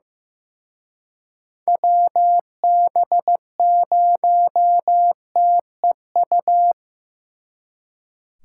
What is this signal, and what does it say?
Morse code, heard as a keyed tone, about 18 wpm, 700 Hz. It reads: WB0TEU
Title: Speed Racing - USA Call Signs 15wpm